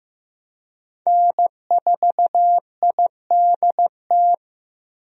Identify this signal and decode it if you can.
Morse code, heard as a keyed tone, about 16 wpm, 700 Hz. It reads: N4IDT
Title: USA Call Signs - 5 Characters - Repeated 3x - 15wpm